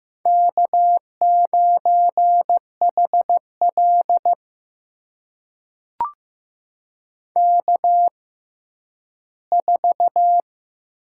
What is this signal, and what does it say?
Morse code, heard as a keyed tone, about 15 wpm, spaced as for 10 wpm, 700 Hz. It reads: K9HL K4
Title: USA Call Signs - 4 Characters - 6x Character Spacing - 15wpm